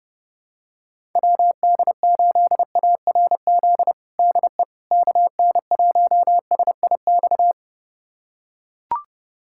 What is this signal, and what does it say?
Morse code, heard as a keyed tone, about 32 wpm, 700 Hz. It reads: WD8ARZ DE KN1SIX
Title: USA Callsign DE Callsign - 6 Characters - 30wpm